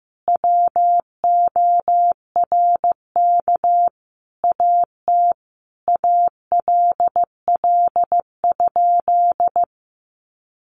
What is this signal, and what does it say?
Morse code, heard as a keyed tone, about 15 wpm, 700 Hz. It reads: WORK AT ALL?
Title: Sentences from Top 1500 Words 15wpm